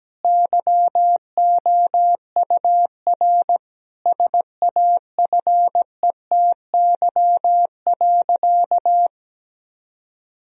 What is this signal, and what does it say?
Morse code, heard as a keyed tone, about 17 wpm, 700 Hz. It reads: YOUR SAFETY.
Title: Sentences from Top 1000 Words - Review Part 4 - 17wpm